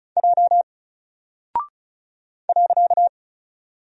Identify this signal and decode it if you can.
Morse code, heard as a keyed tone, about 36 wpm, 700 Hz. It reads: J .
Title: Single Character (Letter:Number:Punctuation:Prosign) Alt 35wpm